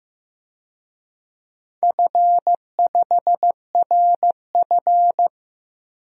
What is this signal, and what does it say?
Morse code, heard as a keyed tone, about 15 wpm, 700 Hz. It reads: F5RF
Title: DXCC-100 Call Signs - 4 Characters - Repeated 2x - 15wpm